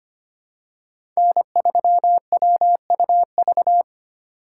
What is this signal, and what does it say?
Morse code, heard as a keyed tone, about 25 wpm, 700 Hz. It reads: N3WUV
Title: USA Call Signs - 5 Characters - 8x Character Spacing - 25wpm